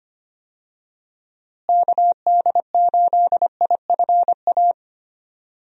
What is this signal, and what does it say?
Morse code, heard as a keyed tone, about 25 wpm, 700 Hz. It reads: KD8IFA
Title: USA Call Signs - 6 Characters - Repeated 3x - 25wpm